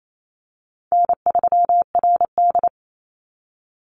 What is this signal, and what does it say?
Morse code, heard as a keyed tone, about 28 wpm, 700 Hz. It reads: N3RD
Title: Easy Call Signs - 28wpm